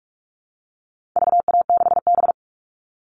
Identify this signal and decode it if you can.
Morse code, heard as a keyed tone, about 44 wpm, 700 Hz. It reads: VA6B
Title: All Cut Numbers - 4-Digit 45wpm